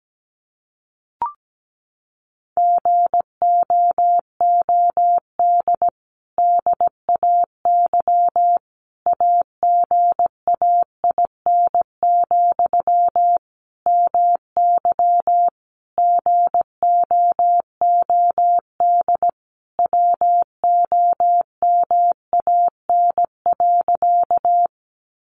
Morse code, heard as a keyed tone, about 17 wpm, 700 Hz. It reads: GOOD DAY AGAIN, MY GOOD WOMAN.
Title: Sentences from Top 1100 Words 17wpm